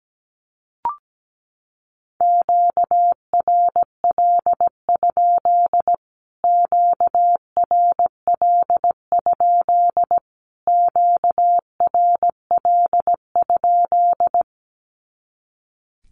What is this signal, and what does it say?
Morse code, heard as a keyed tone, about 17 wpm, 700 Hz. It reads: QRL? QRL? QRL?
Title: Q-Codes Repeated 3x 17wpm